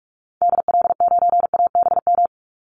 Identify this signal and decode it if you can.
Morse code, heard as a keyed tone, about 44 wpm, 700 Hz. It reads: DL9ABM
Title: Call Signs - 45wpm